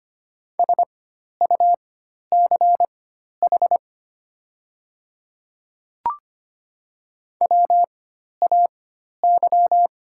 Morse code, heard as a keyed tone, about 25 wpm, spaced as for 12 wpm, 700 Hz. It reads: SUCH WAY
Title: 4x Character Spacing - Top 300 Words - 25wpm